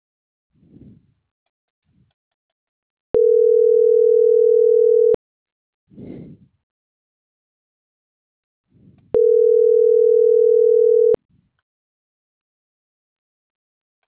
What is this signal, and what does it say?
Ringback tone (North America), 440 + 480 Hz.
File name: CENTRAL MAERSK / out nogueronmsk